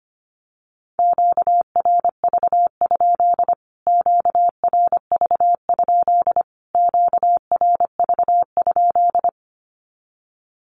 Morse code, heard as a keyed tone, about 25 wpm, 700 Hz. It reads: QRV? QRV? QRV?
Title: Q-Codes Repeated 3x 25wpm